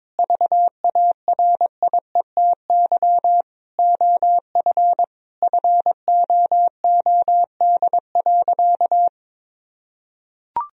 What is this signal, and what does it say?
Morse code, heard as a keyed tone, about 22 wpm, 700 Hz. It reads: VARIETY OF FOOD.
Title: Sentences from Top 1000 Words - Review Part 4 - 22wpm